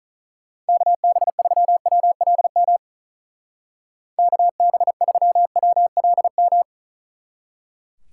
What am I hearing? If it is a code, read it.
Morse code, heard as a keyed tone, about 44 wpm, 700 Hz. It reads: KB3WLM KB3WLM